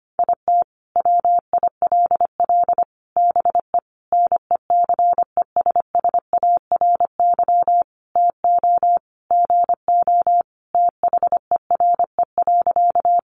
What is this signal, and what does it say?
Morse code, heard as a keyed tone, about 25 wpm, 700 Hz. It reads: IT WILL BE NECESSARY TO GO THERE.
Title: Sentences from Top 800 Words 25wpm